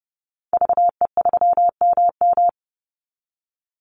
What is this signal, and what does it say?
Morse code, heard as a keyed tone, about 30 wpm, 700 Hz. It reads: VE3MM
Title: Call Signs - 30wpm